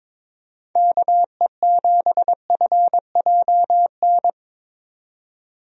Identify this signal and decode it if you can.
Morse code, heard as a keyed tone, about 22 wpm, 700 Hz. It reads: KE7FJN
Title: USA Call Signs - 6 Characters - 22wpm